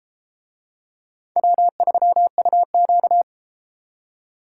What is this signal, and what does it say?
Morse code, heard as a keyed tone, about 30 wpm, 700 Hz. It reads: W3UQ